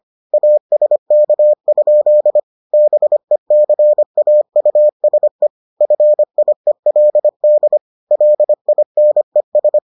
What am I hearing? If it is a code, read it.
Morse code, heard as a keyed tone, about 25 wpm, 600 Hz. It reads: ASK? BECAUSE FIELD LINES